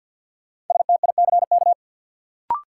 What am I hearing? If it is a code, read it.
Morse code, heard as a keyed tone, about 52 wpm, 700 Hz. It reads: STICK
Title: Sentences from Top 1200 Words - New Word Drill - 50wpm